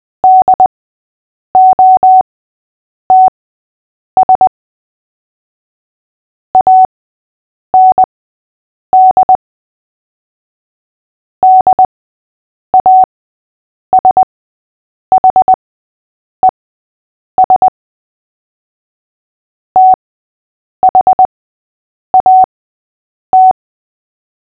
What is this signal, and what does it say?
Morse code, heard as a keyed tone, about 20 wpm, spaced as for 8 wpm, 750 Hz. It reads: DOTS AND DASHES THAT
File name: cw / fists / Short-Story-6-8wpm